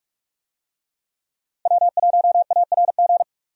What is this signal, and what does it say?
Morse code, heard as a keyed tone, about 46 wpm, 700 Hz. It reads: W1ARG
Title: USA Call Signs - 5 Characters - Repeated 2x - 45wpm